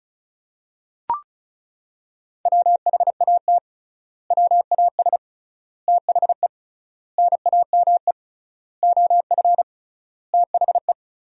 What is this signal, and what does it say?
Morse code, heard as a keyed tone, about 36 wpm, 700 Hz. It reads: WHAT WAS THE NAME OF THE
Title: 3x Word-Spacing - Sentences from Top 300 Words - 35wpm